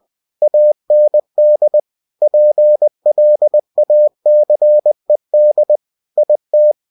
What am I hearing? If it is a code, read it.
Morse code, heard as a keyed tone, about 20 wpm, 600 Hz. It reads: AND PLACED IT
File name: cw-books / winnie-the-pooh / winnie-20wpm-page-0058